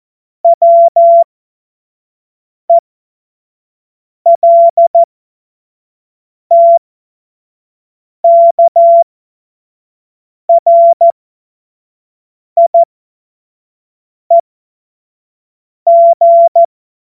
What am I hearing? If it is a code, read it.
Morse code, heard as a keyed tone, about 14 wpm, spaced as for 5 wpm, 680 Hz. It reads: WELTKRIEG
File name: Morsekurs / Witze / Witz 08. November 2022 25 F70